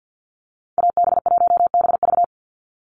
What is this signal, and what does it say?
Morse code, heard as a keyed tone, about 50 wpm, 700 Hz. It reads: AB1BV
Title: USA Call Signs 50wpm